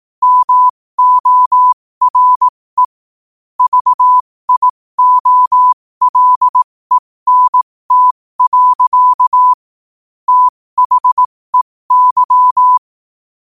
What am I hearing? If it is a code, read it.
Morse code, heard as a keyed tone, about 18 wpm, spaced as for 15 wpm, 1000 Hz. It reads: MORE VIOLENT. THEY
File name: qotdfiles / 15wpm